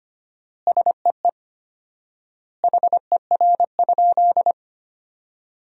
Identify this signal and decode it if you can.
Morse code, heard as a keyed tone, about 25 wpm, 700 Hz. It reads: SEE HER?
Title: 4x Word-Spacing - Sentences from Top 100 Words - 25wpm